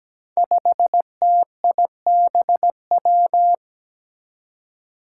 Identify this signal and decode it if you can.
Morse code, heard as a keyed tone, about 17 wpm, 700 Hz. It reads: 5TIBW
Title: Five Challenging Letter-Number Combinations 17wpm